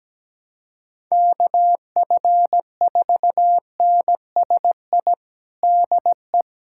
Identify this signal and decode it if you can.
Morse code, heard as a keyed tone, about 17 wpm, 700 Hz. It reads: KF4NSI DE
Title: USA Callsign DE Callsign - 6 Characters - 17wpm